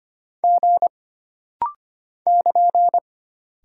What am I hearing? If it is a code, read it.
Morse code, heard as a keyed tone, about 25 wpm, 700 Hz. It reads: G <KN>
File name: Single Character (Letter:Number:Punctuation:Prosign) - Rapid-Fire - 25wpm